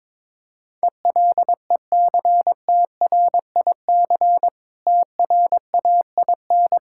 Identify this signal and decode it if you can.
Morse code, heard as a keyed tone, about 21 wpm, 690 Hz. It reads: ELECTRIC TRAIN
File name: Difficult Words with Hints - 22wpm